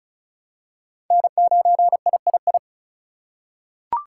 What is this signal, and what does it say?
Morse code, heard as a keyed tone, about 34 wpm, 700 Hz. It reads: N9III
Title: USA Call Signs - 2x Character Spacing - 35wpm